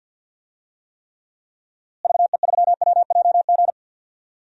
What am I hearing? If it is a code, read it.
Morse code, heard as a keyed tone, about 53 wpm, 700 Hz. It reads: VE3WJG
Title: DXCC-100 Call Signs - 6 Characters - Repeated 2x - 50wpm